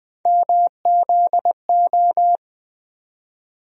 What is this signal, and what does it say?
Morse code, heard as a keyed tone, about 20 wpm, 700 Hz. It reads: MZO
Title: USA Call Sign Postfixes - 3 Letters 20wpm